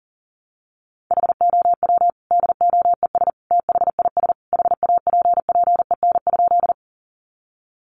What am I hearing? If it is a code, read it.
Morse code, heard as a keyed tone, about 40 wpm, 700 Hz. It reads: HOW DOES THIS HAPPEN?
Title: Sentences from Top 400 Words 40wpm